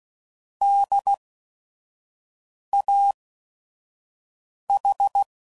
Morse code, heard as a keyed tone, about 16 wpm, spaced as for 5 wpm, 800 Hz. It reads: DAH
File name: Prefixes / By First Letter / Ds